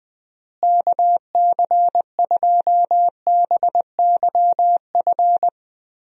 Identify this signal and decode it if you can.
Morse code, heard as a keyed tone, about 20 wpm, 700 Hz. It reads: KC2BYF
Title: USA Call Signs - 6 Characters - Repeated 3x - 20wpm